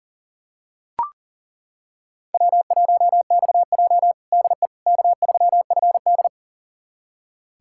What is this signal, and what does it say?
Morse code, heard as a keyed tone, about 40 wpm, 700 Hz. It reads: W1XJ DE K3FD
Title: USA Callsign DE Callsign - 4 Characters - 40wpm